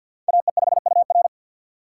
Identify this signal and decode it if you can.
Morse code, heard as a keyed tone, about 52 wpm, 700 Hz. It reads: AE5UR
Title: USA Call Signs - 5 Characters - 2x Character Spacing - 50wpm